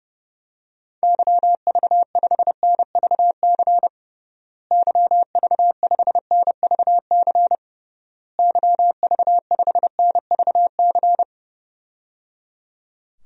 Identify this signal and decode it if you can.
Morse code, heard as a keyed tone, about 30 wpm, 700 Hz. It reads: YV5NVC YV5NVC YV5NVC
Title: DXCC-100 Call Signs - 6 Characters - Repeated 3x - 30wpm